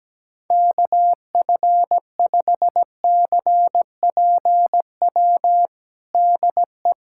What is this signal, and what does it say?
Morse code, heard as a keyed tone, about 17 wpm, 700 Hz. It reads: KF5CPW DE
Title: USA Callsign DE Callsign - 6 Characters - 17wpm